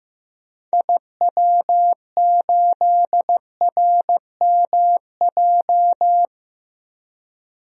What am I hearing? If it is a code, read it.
Morse code, heard as a keyed tone, about 15 wpm, 700 Hz. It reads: IW8RMJ
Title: DXCC-100 Call Signs - 6 Characters - Repeated 2x - 15wpm